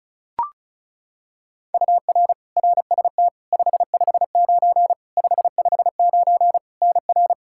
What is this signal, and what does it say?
Morse code, heard as a keyed tone, about 36 wpm, 700 Hz. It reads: UR RST 559 559 NR
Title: Practice QSO - Exchange 1 - Beginning - 35wpm